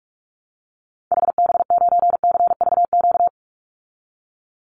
Morse code, heard as a keyed tone, about 44 wpm, 700 Hz. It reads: HB9CVQ
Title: Call Signs Repeated 2x - 45wpm